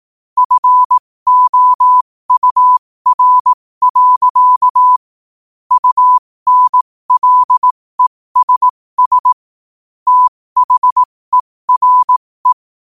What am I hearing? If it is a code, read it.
Morse code, heard as a keyed tone, about 18 wpm, spaced as for 15 wpm, 1000 Hz. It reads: FOUR. UNLESS THERE
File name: qotdfiles / 15wpm